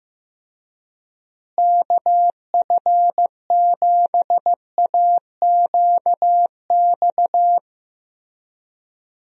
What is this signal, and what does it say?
Morse code, heard as a keyed tone, about 15 wpm, 700 Hz. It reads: KF7AQX
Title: USA Call Signs - 6 Characters - Repeated 3x - 15wpm